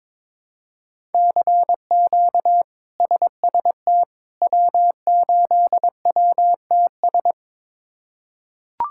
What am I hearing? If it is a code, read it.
Morse code, heard as a keyed tone, about 23 wpm, 700 Hz. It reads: CQ SST W8WTS
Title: K1USN SST QSO Practice - 22wpm